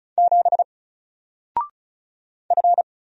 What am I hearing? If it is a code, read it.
Morse code, heard as a keyed tone, about 36 wpm, 700 Hz. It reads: 7 F
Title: Single Character (Letter:Number:Punctuation:Prosign) Alt 35wpm